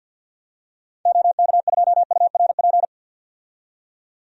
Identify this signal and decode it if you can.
Morse code, heard as a keyed tone, about 48 wpm, 700 Hz. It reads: OK2URP
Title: DXCC-100 Call Signs - 6 Characters - 50wpm